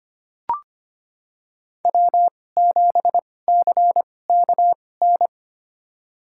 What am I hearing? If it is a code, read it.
Morse code, heard as a keyed tone, about 25 wpm, spaced as for 18 wpm, 700 Hz. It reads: W7CKN
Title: USA Call Signs - 5 Characters - 2x Character Spacing - 25wpm